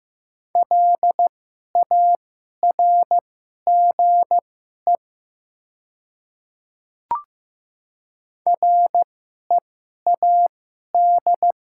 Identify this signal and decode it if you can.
Morse code, heard as a keyed tone, about 15 wpm, spaced as for 11 wpm, 700 Hz. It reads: LARGE READ
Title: 2x Character Spacing - Top 200 Words - 15wpm